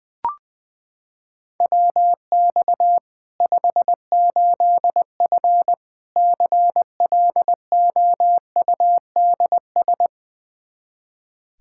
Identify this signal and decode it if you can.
Morse code, heard as a keyed tone, about 20 wpm, 700 Hz. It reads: WX 58F CLOUDS
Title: Practice QSO - Exchange 2 - Info - 20wpm